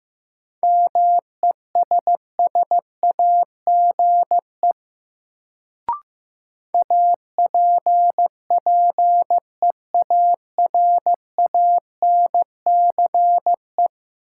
Morse code, heard as a keyed tone, about 15 wpm, 700 Hz. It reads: MESSAGE APPEARANCE
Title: Sentences from Top 1100 Words - New Word Drill Alt - 15wpm